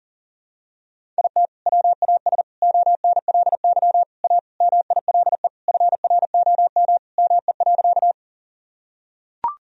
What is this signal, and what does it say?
Morse code, heard as a keyed tone, about 40 wpm, 700 Hz. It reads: IT WAS ONLY A MILE FROM ME.
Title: Sentences from Top 1000 Words - Review Part 1 - 40wpm